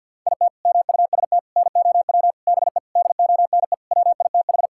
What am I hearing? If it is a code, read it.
Morse code, heard as a keyed tone, about 50 wpm, 700 Hz. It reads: IT MUST NOW BE DONE WITH
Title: Sentences from Top 1000 Words - Review Part 5 - 50wpm